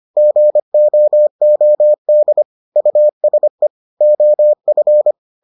Morse code, heard as a keyed tone, about 25 wpm, 600 Hz. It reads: GOOD USE OF